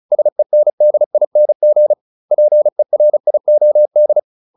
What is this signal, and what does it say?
Morse code, heard as a keyed tone, about 34 wpm, 600 Hz. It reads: SENDING PERIOD